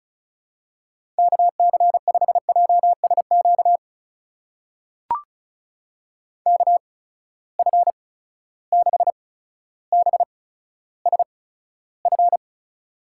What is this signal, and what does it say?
Morse code, heard as a keyed tone, about 36 wpm, spaced as for 15 wpm, 700 Hz. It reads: KC5JSQ KF6BSF
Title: USA Call Signs - 6 Characters - 8x Character Spacing - 35wpm